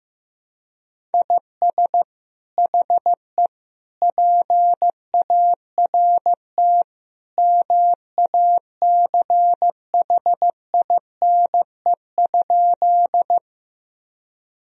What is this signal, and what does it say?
Morse code, heard as a keyed tone, about 16 wpm, 700 Hz. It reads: IS HE PART MACHINE?
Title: Sentences from Top 1000 Words - Review Part 2 - 15wpm